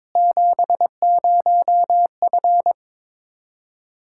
Morse code, heard as a keyed tone, about 22 wpm, 700 Hz. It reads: 70F
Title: Practice QSO - Exchange 2 - Info - 22wpm